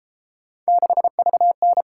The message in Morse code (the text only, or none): 6VN